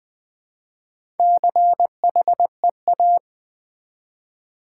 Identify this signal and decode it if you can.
Morse code, heard as a keyed tone, about 20 wpm, 700 Hz. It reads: CHEA
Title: Build A Word - 1 Letter At A Time - 3 to 5-Letter Words - 20wpm